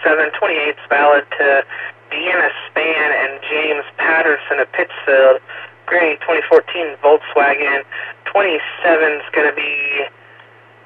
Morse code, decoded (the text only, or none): none